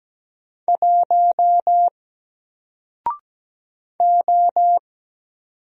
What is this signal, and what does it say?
Morse code, heard as a keyed tone, about 17 wpm, 700 Hz. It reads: 1 O
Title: Single Letter-Number Alt 17wpm